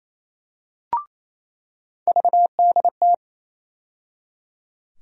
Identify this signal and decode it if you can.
Morse code, heard as a keyed tone, about 28 wpm, 700 Hz. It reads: VDT
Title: USA Call Sign Postfixes - 3 Letters 28wpm